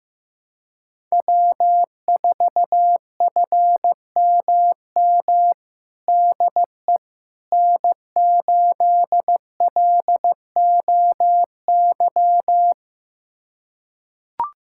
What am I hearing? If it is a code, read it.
Morse code, heard as a keyed tone, about 15 wpm, 700 Hz. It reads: W4FMM DE N8LOY
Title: USA Callsign DE Callsign - 5 Characters - 15wpm